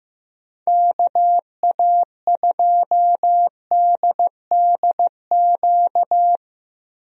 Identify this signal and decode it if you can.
Morse code, heard as a keyed tone, about 15 wpm, 700 Hz. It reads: KA2DDQ
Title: USA Call Signs - 6 Characters - Repeated 3x - 15wpm